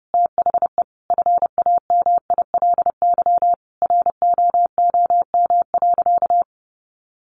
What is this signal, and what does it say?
Morse code, heard as a keyed tone, about 30 wpm, 700 Hz. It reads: THE FAMILY ROOM.
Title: Sentences from Top 1100 Words 30wpm